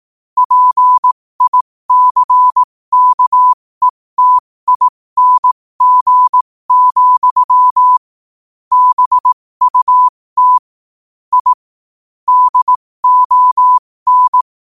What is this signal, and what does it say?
Morse code, heard as a keyed tone, about 18 wpm, spaced as for 15 wpm, 1000 Hz. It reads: PICKETING, BUT I DON
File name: qotdfiles / 15wpm